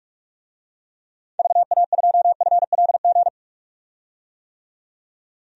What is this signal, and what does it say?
Morse code, heard as a keyed tone, about 46 wpm, 700 Hz. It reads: VA2FLG
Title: DXCC-100 Call Signs - 6 Characters - 45wpm